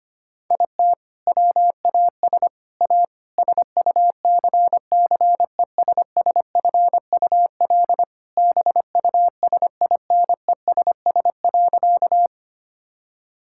Morse code, heard as a keyed tone, about 25 wpm, 700 Hz. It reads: IT WAS A SUCCESSFUL BUSINESS.
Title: Sentences from Top 1000 Words - Review Part 6 - 25wpm